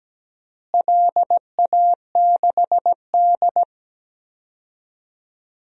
Morse code, heard as a keyed tone, about 17 wpm, 700 Hz. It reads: LA6D
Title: DXCC-100 Call Signs - 4 Characters - Repeated 2x - 17wpm